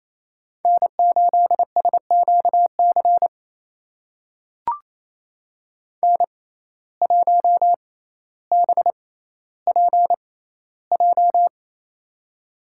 Morse code, heard as a keyed tone, about 28 wpm, spaced as for 14 wpm, 700 Hz. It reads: N8SQC N1BPJ
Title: USA Call Signs - 5 Characters - 6x Character Spacing - 28wpm